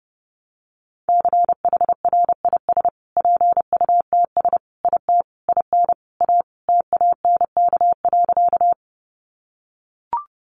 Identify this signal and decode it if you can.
Morse code, heard as a keyed tone, about 30 wpm, 700 Hz. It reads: CHRIS PUTS IT IN A TANK.
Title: Additional Easy Phrases 30wpm